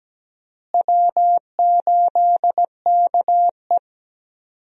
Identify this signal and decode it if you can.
Morse code, heard as a keyed tone, about 17 wpm, 700 Hz. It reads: W8KE
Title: USA Call Signs - 4 Characters - Repeated 3x - 17wpm